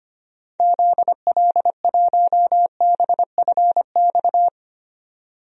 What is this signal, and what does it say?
Morse code, heard as a keyed tone, about 25 wpm, 700 Hz. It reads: ZL1BFX
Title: DXCC-100 Call Signs - 6 Characters - 25wpm